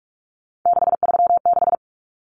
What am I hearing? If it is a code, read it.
Morse code, heard as a keyed tone, about 44 wpm, 700 Hz. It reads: 636
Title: Numbers - 3-Digit 45wpm